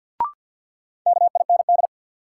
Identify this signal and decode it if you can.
Morse code, heard as a keyed tone, about 53 wpm, 700 Hz. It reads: KIND
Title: Sets of 2 Words - New Word Drill - 50wpm